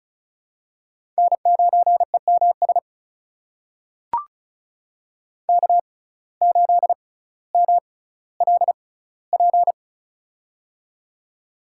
Morse code, heard as a keyed tone, about 36 wpm, spaced as for 18 wpm, 700 Hz. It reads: N9EMS K8MLP
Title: USA Call Signs - 5 Characters - 6x Character Spacing - 35wpm